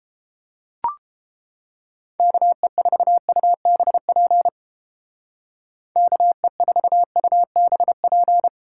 Morse code, heard as a keyed tone, about 31 wpm, 700 Hz. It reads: KE4UBP KE4UBP